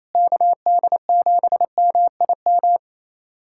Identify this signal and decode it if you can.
Morse code, heard as a keyed tone, about 28 wpm, 700 Hz. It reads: KD7MIM
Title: USA Callsign DE Callsign - 6 Characters - 28wpm